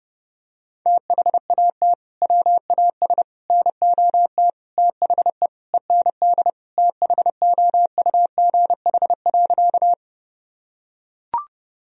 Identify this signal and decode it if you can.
Morse code, heard as a keyed tone, about 32 wpm, 700 Hz. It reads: THAT WAS NOT THE END THOUGH.
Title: Sentences from Top 1000 Words - Review Part 2 - 30wpm